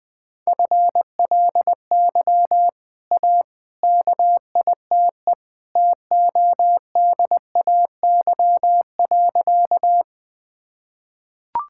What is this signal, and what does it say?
Morse code, heard as a keyed tone, about 20 wpm, 700 Hz. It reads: FLY A KITE TODAY.
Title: Easy Sentences with Hints - 20wpm